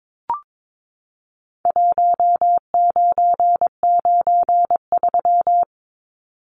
Morse code, heard as a keyed tone, about 22 wpm, 700 Hz. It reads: 1993
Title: Numbers - Contest Style - 4-Digit 22wpm